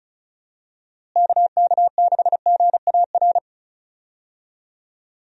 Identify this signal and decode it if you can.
Morse code, heard as a keyed tone, about 36 wpm, 700 Hz. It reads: KK6GAR
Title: USA Call Signs - 6 Characters - Repeated 2x - 35wpm